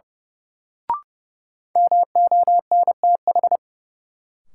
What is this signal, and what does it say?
Morse code, heard as a keyed tone, about 30 wpm, 700 Hz. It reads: MONTH
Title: Sentences from Top 800 Words - New Word Drill - 30wpm